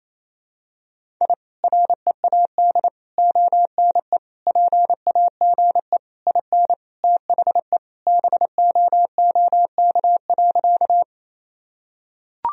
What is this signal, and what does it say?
Morse code, heard as a keyed tone, about 27 wpm, 700 Hz. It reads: I READ ONE PAGE IN THE BOOK.
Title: Sentences from Top 1000 Words - Review Part 1 - 28wpm